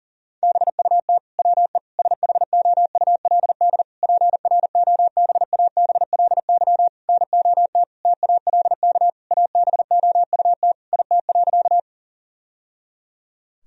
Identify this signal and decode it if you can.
Morse code, heard as a keyed tone, about 37 wpm, 700 Hz. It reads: BUT WE SHOULD PROBABLY NOT TALK ABOUT IT.
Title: Sentences from Top 1000 Words - Review Part 4 - 40wpm